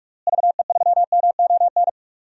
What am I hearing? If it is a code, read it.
Morse code, heard as a keyed tone, about 42 wpm, 700 Hz. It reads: VE3MON
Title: DXCC-100 Call Signs - 6 Characters - Repeated 3x - 45wpm